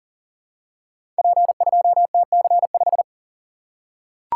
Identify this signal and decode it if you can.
Morse code, heard as a keyed tone, about 41 wpm, 690 Hz. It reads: P2TC5
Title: Code Groups - 5 Letter:Numbers - 40wpm